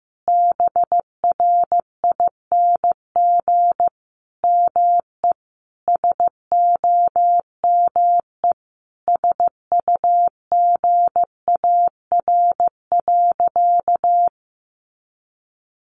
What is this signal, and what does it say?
Morse code, heard as a keyed tone, about 15 wpm, 700 Hz. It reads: BRING ME SOME SUGAR.
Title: Sentences from Top 1200 Words 15wpm